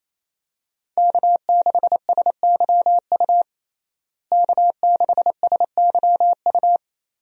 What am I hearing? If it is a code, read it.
Morse code, heard as a keyed tone, about 28 wpm, 700 Hz. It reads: K6SYU K6SYU
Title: USA Call Signs - 5 Characters - Repeated 2x - 28wpm